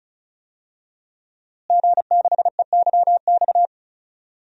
Morse code, heard as a keyed tone, about 36 wpm, 700 Hz. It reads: G6EYX
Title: DXCC-100 Call Signs - 5 Characters - Repeated 3x - 35wpm